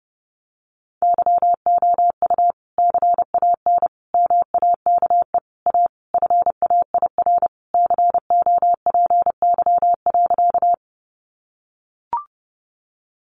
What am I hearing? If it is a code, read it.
Morse code, heard as a keyed tone, about 30 wpm, 700 Hz. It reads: YOU CAN MAKE A FAIR COPY.
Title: Sentences from Top 1400 Words 30wpm